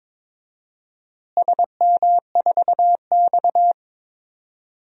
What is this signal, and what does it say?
Morse code, heard as a keyed tone, about 22 wpm, 700 Hz. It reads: SM4X
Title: DXCC-100 Call Signs - 4 Characters - Repeated 3x - 22wpm